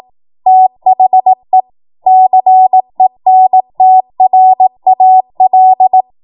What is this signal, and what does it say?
Morse code, heard as a keyed tone, about 18 wpm, 750 Hz. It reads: THE CENTRAL
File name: Morse / 18WPM